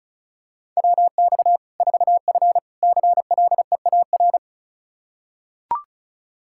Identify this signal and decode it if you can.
Morse code, heard as a keyed tone, about 36 wpm, 700 Hz. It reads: WX 4F CLEAR
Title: Practice QSO - Exchange 2 - Info - 35wpm